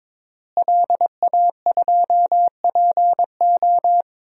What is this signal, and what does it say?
Morse code, heard as a keyed tone, about 22 wpm, 700 Hz. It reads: LA2PO
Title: DXCC-100 Call Signs - 5 Characters - Repeated 3x - 22wpm